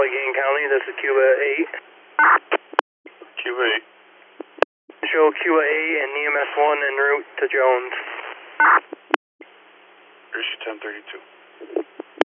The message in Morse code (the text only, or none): none